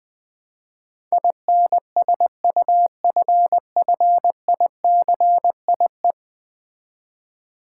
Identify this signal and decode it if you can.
Morse code, heard as a keyed tone, about 20 wpm, 700 Hz. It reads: INSUFFICIE
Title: Build A Word - 5 Letters At A Time - 12-Letter Words - 20wpm